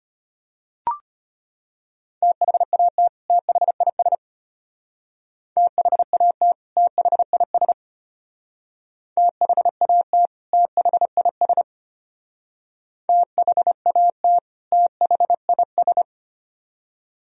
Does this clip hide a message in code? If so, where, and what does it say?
Yes: Morse code from 2.2 to 16.0 s, heard as a keyed tone, about 33 wpm, 700 Hz. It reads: THAT THIS THAT THIS THAT THIS THAT THIS